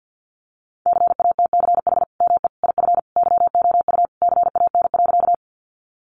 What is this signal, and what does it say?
Morse code, heard as a keyed tone, about 50 wpm, 700 Hz. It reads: CATCH ME IF YOU CAN.
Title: Sentences from Top 1500 Words 50wpm